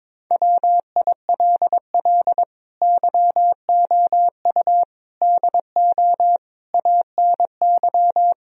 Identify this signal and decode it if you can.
Morse code, heard as a keyed tone, about 22 wpm, 700 Hz. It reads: WILL YOU DO ANY
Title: Sentences from Top 1000 Words - Review Part 3 - 22wpm